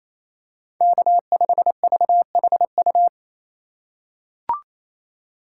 Morse code, heard as a keyed tone, about 28 wpm, 700 Hz. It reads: K5VHU
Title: USA Call Signs - 5 Characters - 2x Character Spacing - 28wpm